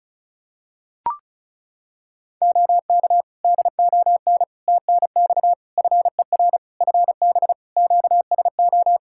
Morse code, heard as a keyed tone, about 36 wpm, 700 Hz. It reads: OK DON TNX FER FB QSO
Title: Practice QSO - Exchange 4 - Ending - 35wpm